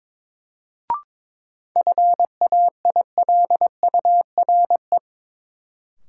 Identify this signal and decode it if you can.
Morse code, heard as a keyed tone, about 22 wpm, 700 Hz. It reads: FAILURE
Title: Sentences from Top 900 Words - New Word Drill - 22wpm